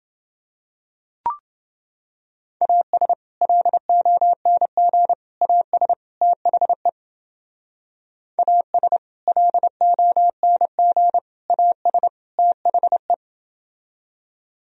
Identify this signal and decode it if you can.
Morse code, heard as a keyed tone, about 28 wpm, 700 Hz. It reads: AS LONG AS THE AS LONG AS THE